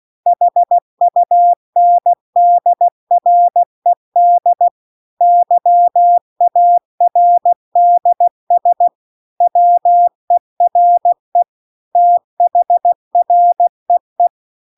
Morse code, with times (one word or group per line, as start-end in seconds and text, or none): 0.3-4.7 HUNDRED
5.2-8.9 YARDS
9.4-11.4 WERE
12.0-14.3 THREE